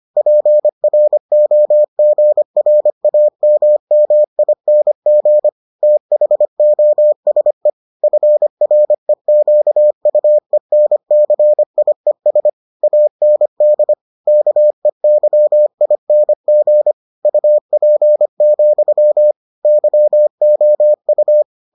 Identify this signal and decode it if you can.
Morse code, heard as a keyed tone, about 25 wpm, 600 Hz. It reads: PROGRAMMING THOSE FREQUENCIES AND KEYING UP, YOU